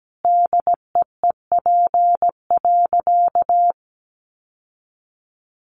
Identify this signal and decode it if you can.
Morse code, heard as a keyed tone, about 17 wpm, 700 Hz. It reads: DEEP.
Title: Sentences from Top 1100 Words 17wpm